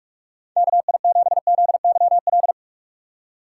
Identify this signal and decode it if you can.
Morse code, heard as a keyed tone, about 46 wpm, 700 Hz. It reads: KI7ZYL
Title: USA Call Signs - 6 Characters - 3x Character Spacing - 45wpm